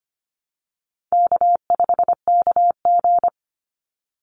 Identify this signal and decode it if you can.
Morse code, heard as a keyed tone, about 25 wpm, 700 Hz. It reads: K5KG
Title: Easy Call Signs Repeated 3x - 25wpm